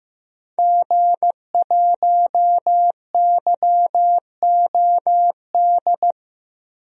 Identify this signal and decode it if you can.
Morse code, heard as a keyed tone, about 15 wpm, 700 Hz. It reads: G1YOD
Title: DXCC-100 Call Signs - 5 Characters - 15wpm